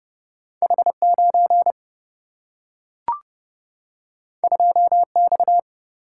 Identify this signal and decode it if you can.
Morse code, heard as a keyed tone, about 30 wpm, 700 Hz. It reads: H 9 2 X
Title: Single-Letter Single-Number 30wpm